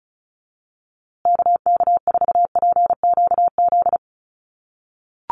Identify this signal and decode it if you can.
Morse code, heard as a keyed tone, about 34 wpm, 700 Hz. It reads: KK4PQZ
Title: USA Call Signs Repeated 2x 35wpm